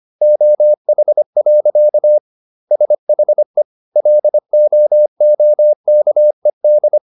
Morse code, heard as a keyed tone, about 25 wpm, 600 Hz. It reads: OH. SHE LOOKED